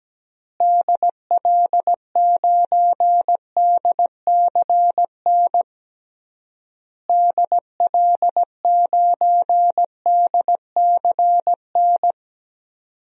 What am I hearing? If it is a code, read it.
Morse code, heard as a keyed tone, about 17 wpm, 700 Hz. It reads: DL9DCN DL9DCN
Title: DXCC-100 Call Signs - 6 Characters - Repeated 3x - 17wpm